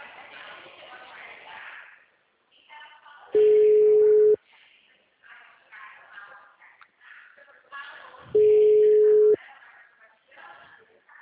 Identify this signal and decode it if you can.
Ringback tone (Europe, 425 Hz), with speech.